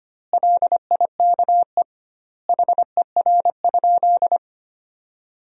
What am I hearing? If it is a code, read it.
Morse code, heard as a keyed tone, about 25 wpm, 700 Hz. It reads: LIKE HER?
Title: 2x Word-Spacing - Sentences from Top 100 Words - 25wpm